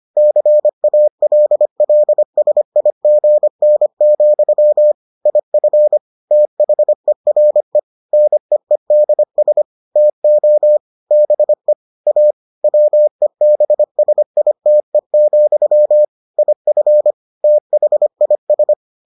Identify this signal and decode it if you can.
Morse code, heard as a keyed tone, about 25 wpm, 600 Hz. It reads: CALLSIGN, IF THERE NEEDS TO BE A WEBSITE, IF THIS